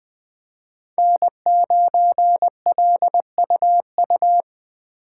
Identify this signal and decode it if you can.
Morse code, heard as a keyed tone, about 20 wpm, 700 Hz. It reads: N9LUU
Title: USA Call Signs - 5 Characters - 4x Character Spacing - 20wpm